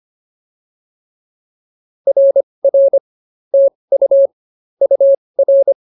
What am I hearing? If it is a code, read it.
Morse code, heard as a keyed tone, about 25 wpm, spaced as for 20 wpm, 590 Hz. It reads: RR TU UR